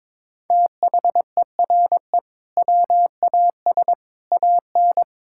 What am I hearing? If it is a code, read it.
Morse code, heard as a keyed tone, about 22 wpm, 700 Hz. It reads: THERE WAS AN
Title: Sentences from Top 1000 Words - Review Part 6 - 22wpm